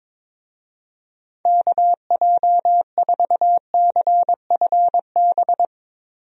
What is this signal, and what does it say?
Morse code, heard as a keyed tone, about 22 wpm, 700 Hz. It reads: KJ4CFB
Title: USA Call Signs - 6 Characters - Repeated 2x - 22wpm